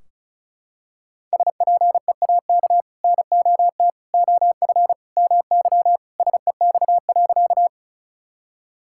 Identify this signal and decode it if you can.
Morse code, heard as a keyed tone, about 36 wpm, 700 Hz. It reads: SPEAK NOT OF MY SEX.
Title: Sentences from Top 1600 Words 35wpm